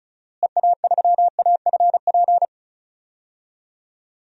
Morse code, heard as a keyed tone, about 34 wpm, 700 Hz. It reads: EA3AFP
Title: DXCC-100 Call Signs - 6 Characters - Repeated 2x - 35wpm